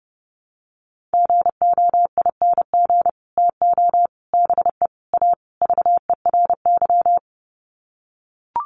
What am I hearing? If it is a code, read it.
Morse code, heard as a keyed tone, about 30 wpm, 700 Hz. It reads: GOING TO BE A VERY
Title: Sets of 5 Words Encore 30wpm